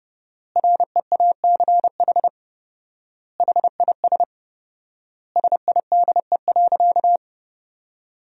Morse code, heard as a keyed tone, about 32 wpm, 700 Hz. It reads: REACH HIS SIDE.
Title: 4x Word-Spacing - Sentences from Top 400 Words - 30wpm